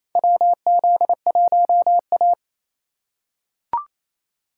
Morse code, heard as a keyed tone, about 28 wpm, 700 Hz. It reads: WZ1A
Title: USA Call Signs - 4 Characters - 2x Character Spacing - 28wpm